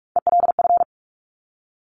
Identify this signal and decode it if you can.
Morse code, heard as a keyed tone, about 46 wpm, 700 Hz. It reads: ELF
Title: Top Three Letter Combinations in Words 45wpm